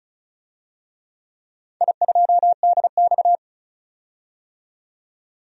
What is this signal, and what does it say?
Morse code, heard as a keyed tone, about 36 wpm, 700 Hz. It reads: I2DX
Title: DXCC-100 Call Signs - 4 Characters - 35wpm